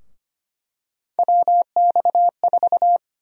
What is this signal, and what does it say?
Morse code, heard as a keyed tone, about 25 wpm, 700 Hz. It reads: WX4
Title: USA Call Sign Prefixes - 25wpm